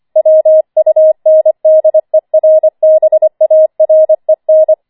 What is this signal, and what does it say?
Morse code, heard as a keyed tone, about 25 wpm, 630 Hz. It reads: WUNDERBAREN
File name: 25wpm